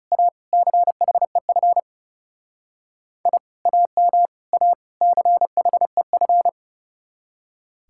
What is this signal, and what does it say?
Morse code, heard as a keyed tone, about 40 wpm, 700 Hz. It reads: A CHEF I AM A CHEF